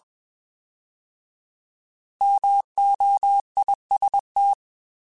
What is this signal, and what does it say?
Morse code, heard as a keyed tone, about 21 wpm, 800 Hz. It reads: MOIST
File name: Alphabetical / Alphabetical mp3s / Ms @20